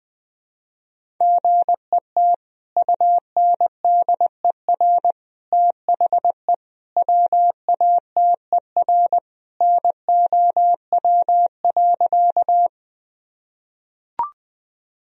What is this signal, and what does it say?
Morse code, heard as a keyed tone, about 21 wpm, 700 Hz. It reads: GET UNDER THE WATER NOW.
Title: Sentences from Top 1000 Words - Review Part 1 - 20wpm